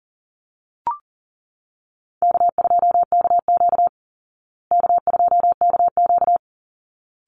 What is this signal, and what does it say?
Morse code, heard as a keyed tone, about 40 wpm, 700 Hz. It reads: K2KQ K2KQ
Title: Easy Call Signs Repeated 2x - 40wpm